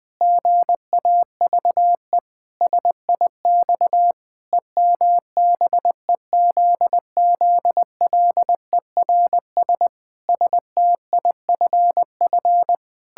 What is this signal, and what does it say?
Morse code, heard as a keyed tone, about 20 wpm, 700 Hz. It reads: GAVE SIX EMBEZZLERS STIFF
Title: Pangram Challenge v2 - 20wpm